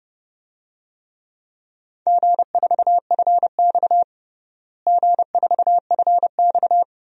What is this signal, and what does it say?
Morse code, heard as a keyed tone, about 30 wpm, 700 Hz. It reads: G4FX G4FX
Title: DXCC-100 Call Signs - 4 Characters - Repeated 3x - 30wpm